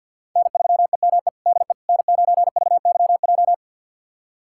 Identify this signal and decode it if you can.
Morse code, heard as a keyed tone, about 46 wpm, 690 Hz. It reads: N3EME DE N9VYJ
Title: USA Callsign DE Callsign - 5 Characters - 50wpm